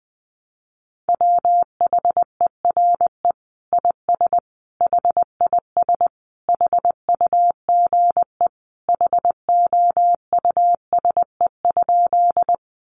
Morse code, heard as a keyed tone, about 20 wpm, 700 Hz. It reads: WHERE IS HIS HUGE HOUSE?
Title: Sentences from Top 1100 Words 20wpm